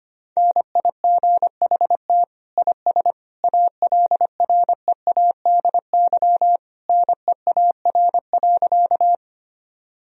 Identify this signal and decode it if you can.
Morse code, heard as a keyed tone, about 25 wpm, 700 Hz. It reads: NIGHT IS ALREADY NEAR.
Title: Sentences from Top 1000 Words - Review Part 1 - 25wpm